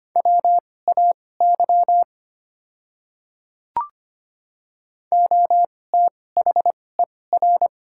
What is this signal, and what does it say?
Morse code, heard as a keyed tone, about 25 wpm, spaced as for 18 wpm, 690 Hz. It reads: WAY OTHER
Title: 2x Character Spacing - Top 100 Words - 25wpm